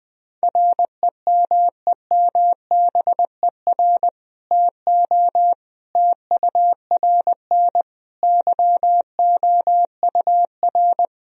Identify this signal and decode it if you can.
Morse code, heard as a keyed tone, about 21 wpm, 700 Hz. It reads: REMEMBER TO TURN YOUR
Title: Sentences from Top 1000 Words - Review Part 3 - 20wpm